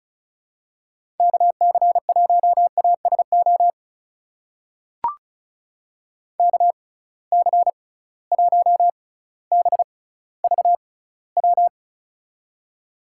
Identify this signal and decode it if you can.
Morse code, heard as a keyed tone, about 36 wpm, spaced as for 18 wpm, 700 Hz. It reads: KC1ASO KC1BVW
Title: USA Call Signs - 6 Characters - 6x Character Spacing - 35wpm